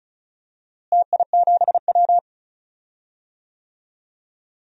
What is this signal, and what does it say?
Morse code, heard as a keyed tone, about 36 wpm, 700 Hz. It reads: TI7W
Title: DXCC-100 Call Signs - 4 Characters - Repeated 3x - 35wpm